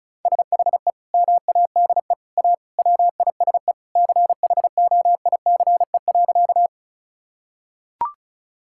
Morse code, heard as a keyed tone, about 36 wpm, 700 Hz. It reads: SHE MADE A WISE CHOICE.
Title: Sentences from Top 1600 Words 35wpm